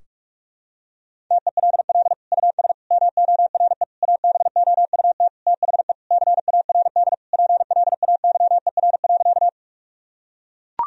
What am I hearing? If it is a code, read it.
Morse code, heard as a keyed tone, about 42 wpm, 700 Hz. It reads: TELL US MORE ABOUT THE CARD PLAYER.
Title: Sentences from Top 1000 Words - Review Part 4 - 45wpm